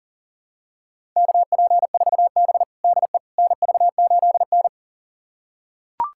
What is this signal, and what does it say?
Morse code, heard as a keyed tone, about 40 wpm, 700 Hz. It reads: KP4B DE NV8N
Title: USA Callsign DE Callsign - 4 Characters - 40wpm